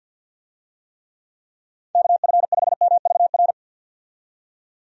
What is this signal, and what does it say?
Morse code, heard as a keyed tone, about 51 wpm, 700 Hz. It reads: KF5MVR
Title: USA Call Signs - 6 Characters - 50wpm